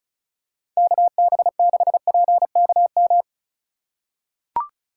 Morse code, heard as a keyed tone, about 34 wpm, 700 Hz. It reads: KB6PKM
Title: USA Call Signs - 6 Characters - 4x Character Spacing - 35wpm